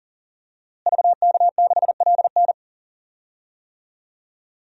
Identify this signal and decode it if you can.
Morse code, heard as a keyed tone, about 40 wpm, 700 Hz. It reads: VK6LN
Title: DXCC-100 Call Signs - 5 Characters - Repeated 3x - 40wpm